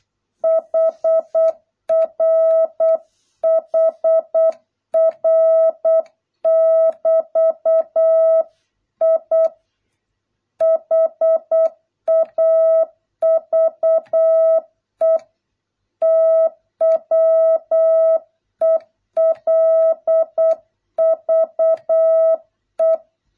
Morse code, heard as a keyed tone, about 8 wpm, 650 Hz. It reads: HR HR <BT> I HAVE TWELVE